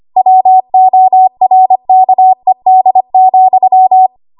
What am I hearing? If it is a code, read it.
Morse code, heard as a keyed tone, about 25 wpm, 750 Hz. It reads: WORKED,
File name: Archive / 25 WPM / 25WPM